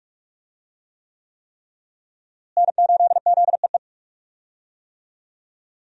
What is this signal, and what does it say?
Morse code, heard as a keyed tone, about 41 wpm, 700 Hz. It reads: N8ZEE
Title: USA Call Signs - 5 Characters - Repeated 3x - 45wpm